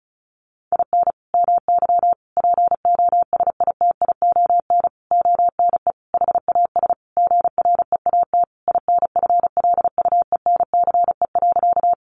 Morse code, heard as a keyed tone, about 34 wpm, 700 Hz. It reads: IN MY POSITION ONE HAS GREAT INFLUENCE.
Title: Sentences from Top 1100 Words 35wpm